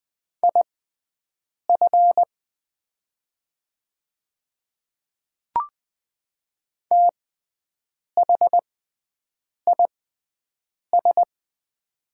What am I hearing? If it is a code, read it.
Morse code, heard as a keyed tone, about 20 wpm, spaced as for 7 wpm, 700 Hz. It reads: IF THIS
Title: 6x Character Spacing - Top 200 Words - 20wpm